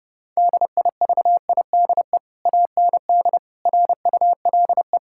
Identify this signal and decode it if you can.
Morse code, heard as a keyed tone, about 30 wpm, 700 Hz. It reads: DIVIDE AND RULE
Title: Sentences from Top 1600 Words 30wpm